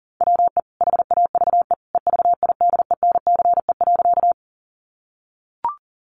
Morse code, heard as a keyed tone, about 40 wpm, 700 Hz. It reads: WE HAVE EVIDENCE.
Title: Sentences from Top 1300 Words 40wpm